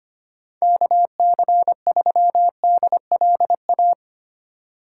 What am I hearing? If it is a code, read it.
Morse code, heard as a keyed tone, about 25 wpm, 700 Hz. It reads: KC3DLA
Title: USA Call Signs - 6 Characters - 8x Character Spacing - 25wpm